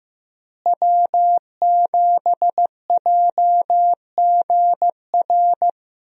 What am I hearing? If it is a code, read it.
Morse code, heard as a keyed tone, about 15 wpm, 700 Hz. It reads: W7JGR
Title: USA Call Signs - 5 Characters - Repeated 3x - 15wpm